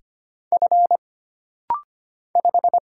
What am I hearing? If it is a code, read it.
Morse code, heard as a keyed tone, about 25 wpm, 700 Hz. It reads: F 5
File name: Single Character (Letter:Number:Punctuation:Prosign) - Rapid-Fire - 25wpm